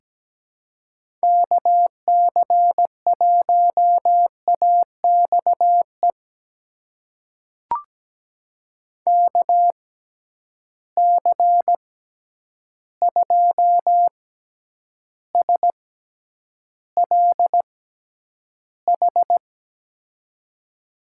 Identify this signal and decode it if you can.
Morse code, heard as a keyed tone, about 17 wpm, spaced as for 9 wpm, 700 Hz. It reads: KC1AXE KC2SLH
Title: USA Call Signs - 6 Characters - 6x Character Spacing - 17wpm